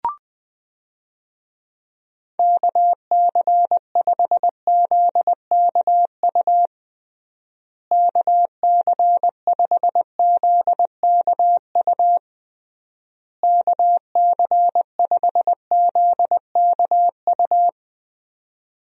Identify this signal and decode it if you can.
Morse code, heard as a keyed tone, about 20 wpm, 700 Hz. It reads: KC5ZKU KC5ZKU KC5ZKU
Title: USA Call Signs - 6 Characters - Repeated 3x - 20wpm